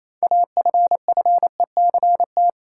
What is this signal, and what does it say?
Morse code, heard as a keyed tone, about 28 wpm, 700 Hz. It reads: AFFECT
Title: Sentences from Top 900 Words - New Word Drill - 28wpm